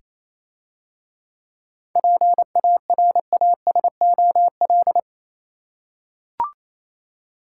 Morse code, heard as a keyed tone, about 27 wpm, 700 Hz. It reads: PARASOL
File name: Easy Words - Speed-Racing with Hints - 22wpm